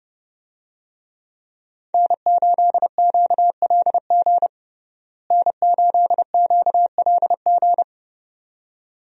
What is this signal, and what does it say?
Morse code, heard as a keyed tone, about 29 wpm, 700 Hz. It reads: N8QLG N8QLG
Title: USA Call Signs - 5 Characters - Repeated 2x - 30wpm